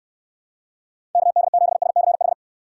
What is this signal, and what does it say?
Morse code, heard as a keyed tone, about 69 wpm, 700 Hz. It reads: KD7SCH